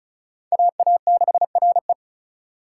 Morse code, heard as a keyed tone, about 36 wpm, 700 Hz. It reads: AA6RE
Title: K1USN SST QSO Practice - 35wpm